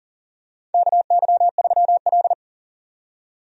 Morse code, heard as a keyed tone, about 40 wpm, 700 Hz. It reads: KY3L
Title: USA Call Signs - 4 Characters - 40wpm